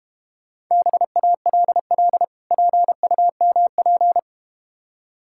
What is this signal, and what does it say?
Morse code, heard as a keyed tone, about 31 wpm, 700 Hz. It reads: BALL PUMP
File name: practice / 25wpm / Difficult Words - Speed-Racing with Hints - 25wpm